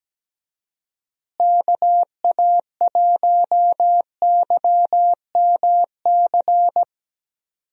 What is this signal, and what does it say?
Morse code, heard as a keyed tone, about 17 wpm, 700 Hz. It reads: KA1YMC
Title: USA Call Signs - 6 Characters - Repeated 3x - 17wpm